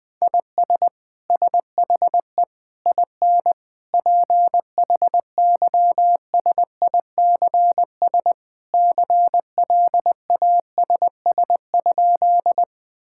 Morse code, heard as a keyed tone, about 20 wpm, 700 Hz. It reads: IS SHE IN PHYSICS CLASS?
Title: Sentences from Top 1000 Words - Review Part 5 - 20wpm